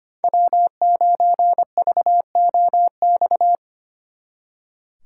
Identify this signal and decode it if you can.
Morse code, heard as a keyed tone, about 31 wpm, 700 Hz. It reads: W9VOX
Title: Speed Racing - USA 1x3 Call Signs 25wpm